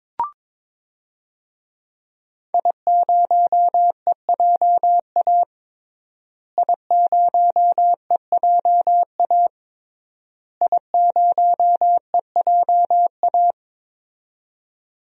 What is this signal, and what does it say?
Morse code, heard as a keyed tone, about 23 wpm, 700 Hz. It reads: I0EJA I0EJA I0EJA
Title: DXCC-100 Call Signs - 5 Characters - Repeated 3x - 22wpm